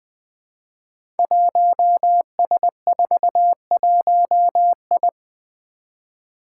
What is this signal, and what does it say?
Morse code, heard as a keyed tone, about 20 wpm, 700 Hz. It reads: 1S41I
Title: Five Challenging Letter-Number Combinations 20wpm